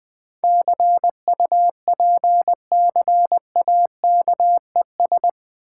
Morse code, heard as a keyed tone, about 21 wpm, 690 Hz. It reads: CUPCAKES
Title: Difficult Words with Hints - 20wpm